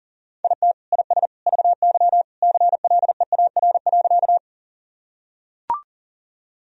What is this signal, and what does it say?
Morse code, heard as a keyed tone, about 40 wpm, 700 Hz. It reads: IT IS VY CLEAR.
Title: Sentences from Top 500 Words and Abbreviations 40wpm